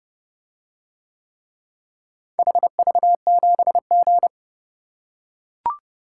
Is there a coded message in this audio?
Yes: Morse code from 2.4 to 4.3 s, heard as a keyed tone, about 29 wpm, 700 Hz. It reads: HV7G